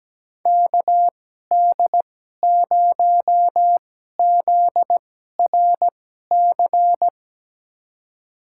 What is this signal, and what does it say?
Morse code, heard as a keyed tone, about 17 wpm, spaced as for 12 wpm, 700 Hz. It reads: KD0ZRC
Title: USA Call Signs - 6 Characters - 2x Character Spacing - 17wpm